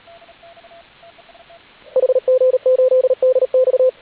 Morse code, heard as a keyed tone, about 38 wpm, 500 Hz, with radio noise. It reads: HG8DX